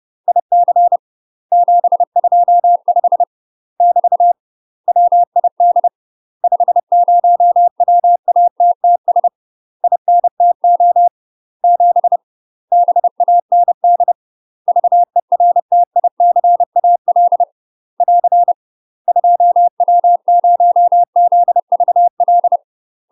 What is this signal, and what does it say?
Morse code, heard as a keyed tone, about 30 wpm, 700 Hz. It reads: IC 725 <BT> WID 50 WATTS INTO 7 BAND VERTICAL <AR> 2W0ZVL
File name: UK MASTER QSOs TRAINING / 30WPM / UK MASTER QSOs 0005